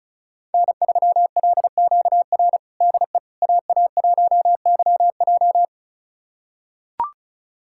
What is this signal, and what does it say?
Morse code, heard as a keyed tone, about 34 wpm, 700 Hz. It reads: N3LQR DE AA1YJ
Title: USA Callsign DE Callsign - 5 Characters - 35wpm